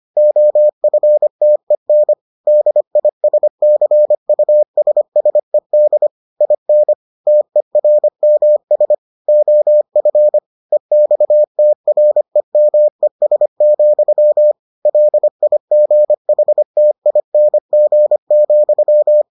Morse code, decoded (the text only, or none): OFTEN DISCUSSED IN TERMS OF EXTREMES, LIGHTING,